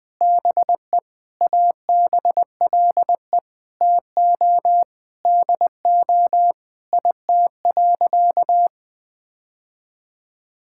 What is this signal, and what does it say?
Morse code, heard as a keyed tone, about 20 wpm, 700 Hz. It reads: BE ABLE TO DO IT.
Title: Sentences with 4-Word Phrases - 20wpm